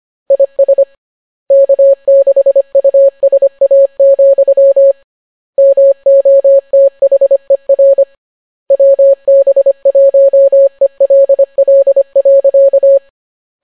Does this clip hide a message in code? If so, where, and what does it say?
Yes: Morse code from 0.3 to 13.0 s, heard as a keyed tone, about 25 wpm, 550 Hz. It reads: IS K6USA, MOTHER WB1ELL.